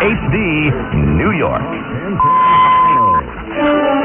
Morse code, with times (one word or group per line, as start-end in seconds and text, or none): none